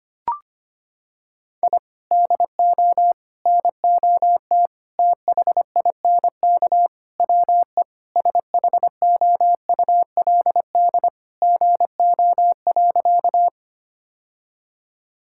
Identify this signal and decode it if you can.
Morse code, heard as a keyed tone, about 25 wpm, 700 Hz. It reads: I DO NOT THINK WE SHOULD GO.
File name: practice / 25wpm / Sentences with 4-Word Phrases - 25wpm